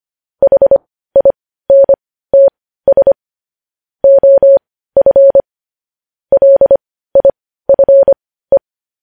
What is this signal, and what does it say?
Morse code, heard as a keyed tone, about 25 wpm, spaced as for 15 wpm, 550 Hz. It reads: HINTS OF LIFE
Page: news headlines in morse code at 15 wpm